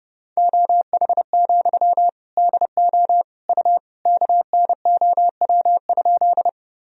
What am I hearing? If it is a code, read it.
Morse code, heard as a keyed tone, about 32 wpm, 700 Hz. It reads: OH, DO U KNOW?
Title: Sentences from Top 500 Words and Abbreviations 30wpm